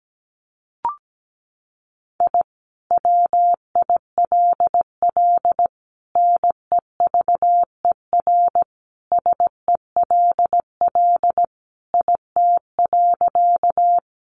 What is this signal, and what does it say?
Morse code, heard as a keyed tone, about 17 wpm, 700 Hz. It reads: I WILL NEVER SELL IT.
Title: Sentences from Top 1100 Words 17wpm